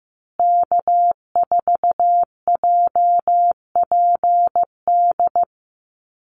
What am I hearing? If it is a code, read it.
Morse code, heard as a keyed tone, about 15 wpm, 700 Hz. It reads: K4JPD
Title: Call Signs Repeated 2x - 15wpm